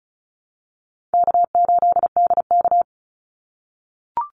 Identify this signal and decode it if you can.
Morse code, heard as a keyed tone, about 34 wpm, 700 Hz. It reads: K8DK
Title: Call Signs - 35wpm